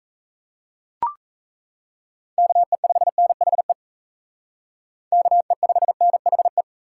Morse code, heard as a keyed tone, about 35 wpm, 700 Hz. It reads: KE5NHE KE5NHE
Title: Speed Racing - USA 2x3 Call Signs 28wpm